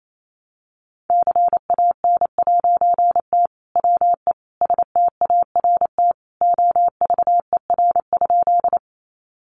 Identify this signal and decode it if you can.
Morse code, heard as a keyed tone, about 28 wpm, 700 Hz. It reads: CAN'T WE START OVER?
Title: A Sound Of Thunder 28wpm